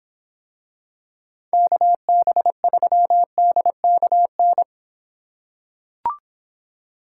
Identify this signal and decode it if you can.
Morse code, heard as a keyed tone, about 26 wpm, 700 Hz. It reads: KB3DKN